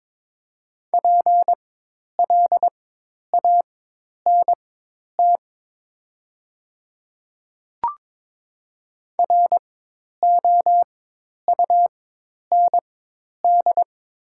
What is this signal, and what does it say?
Morse code, heard as a keyed tone, about 22 wpm, spaced as for 10 wpm, 700 Hz. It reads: PLANT ROUND
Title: 4x Character Spacing - Top 300 Words - 22wpm